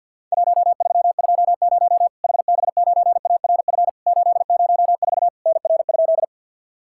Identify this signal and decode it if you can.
Morse code, heard as a keyed tone, about 50 wpm, 700 Hz. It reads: 1320 HB9ARF 804 NR?
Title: Worked All Europe DX Contest - QTC Exchange with Repeats - 50wpm